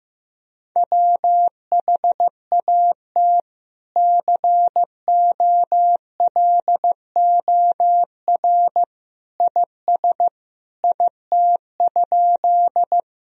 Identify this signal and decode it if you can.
Morse code, heard as a keyed tone, about 15 wpm, 700 Hz. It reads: WHAT COLOR IS IT?
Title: Sentences from Top 1000 Words - Review Part 2 - 15wpm